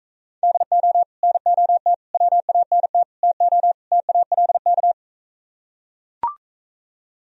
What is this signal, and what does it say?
Morse code, heard as a keyed tone, about 42 wpm, 700 Hz. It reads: DO NOT WANT TO TALK